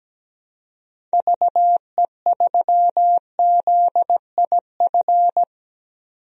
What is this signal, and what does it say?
Morse code, heard as a keyed tone, about 17 wpm, 700 Hz. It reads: VE3ZIF
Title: DXCC-100 Call Signs - 6 Characters - 17wpm